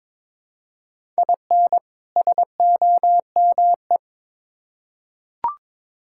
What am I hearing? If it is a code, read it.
Morse code, heard as a keyed tone, about 22 wpm, 700 Hz. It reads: IN SOME
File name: Build A Phrase - 2-Word Phrases - 22wpm